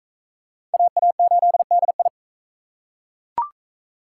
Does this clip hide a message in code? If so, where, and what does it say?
Yes: Morse code from 0.7 to 2.1 s, heard as a keyed tone, about 40 wpm, 700 Hz. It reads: AA8DI